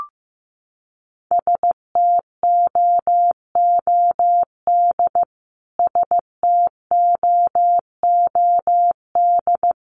Morse code, heard as a keyed tone, about 15 wpm, 700 Hz. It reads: STOOD STOOD
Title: Top 500 Words Repeated 15wpm